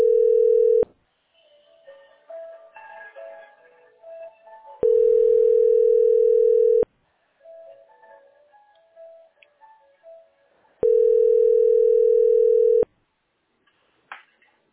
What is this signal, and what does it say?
Ringback tone (North America), 440 + 480 Hz.